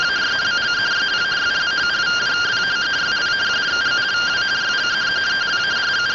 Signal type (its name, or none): two-tone teleprinter signal